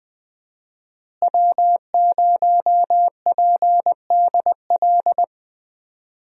Morse code, heard as a keyed tone, about 20 wpm, 700 Hz. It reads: W0PDL
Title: USA Call Signs - 5 Characters - Repeated 3x - 20wpm